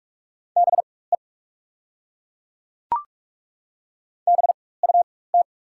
Morse code, heard as a keyed tone, about 47 wpm, spaced as for 21 wpm, 690 Hz. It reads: BE BUT
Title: 4x Character Spacing - Top 200 Words - 45wpm